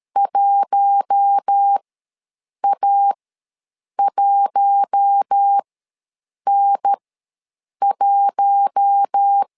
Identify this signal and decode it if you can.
Morse code, heard as a keyed tone, about 13 wpm, spaced as for 7 wpm, 800 Hz. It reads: 1A1N1
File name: LEZIONI / 4 Q1 ws 12wpm 2129